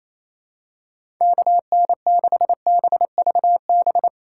Morse code, heard as a keyed tone, about 28 wpm, 700 Hz. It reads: KN6BVB
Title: USA Call Signs - 2x Character Spacing - 28wpm